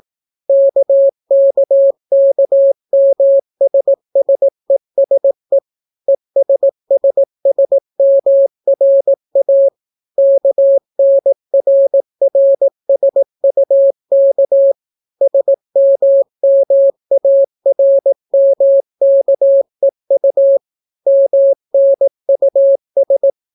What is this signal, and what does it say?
Morse code, heard as a keyed tone, about 18 wpm, 550 Hz. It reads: KKKMSSESE ESSSMRA KNRRSUK SMMARMKEU MNUS